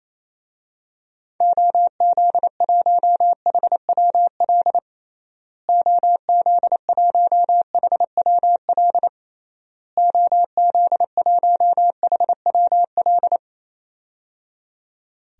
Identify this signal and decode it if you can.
Morse code, heard as a keyed tone, about 28 wpm, 700 Hz. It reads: OZ1HWL OZ1HWL OZ1HWL
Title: DXCC-100 Call Signs - 6 Characters - Repeated 3x - 28wpm